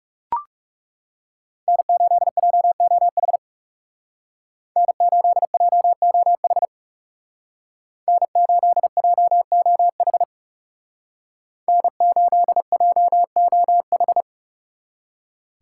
Morse code, heard as a keyed tone, about 37 wpm, 700 Hz. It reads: N8JOH N8JOH N8JOH N8JOH
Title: Speed Racing - USA 1x3 Call Signs 30wpm